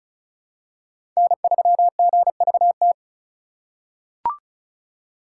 Morse code, heard as a keyed tone, about 34 wpm, 700 Hz. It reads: N3GVT
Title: USA Call Signs - 5 Characters - 2x Character Spacing - 35wpm